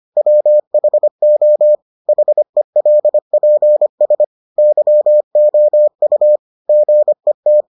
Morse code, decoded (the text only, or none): WHO HELPS YOU GET